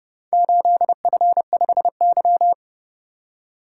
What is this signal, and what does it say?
Morse code, heard as a keyed tone, about 30 wpm, 700 Hz. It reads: 8F5Y